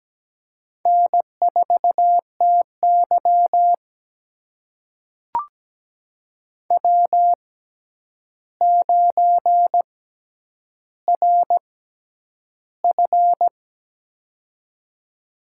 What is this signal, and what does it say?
Morse code, heard as a keyed tone, about 17 wpm, spaced as for 9 wpm, 700 Hz. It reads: N4TY W9RF
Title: USA Call Signs - 4 Characters - 6x Character Spacing - 17wpm